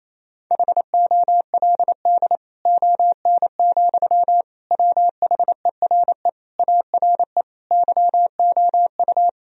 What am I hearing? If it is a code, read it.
Morse code, heard as a keyed tone, about 28 wpm, 700 Hz. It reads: HOLD ON, WHERE ARE YOU
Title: Sentences from Top 1000 Words - Review Part 2 - 28wpm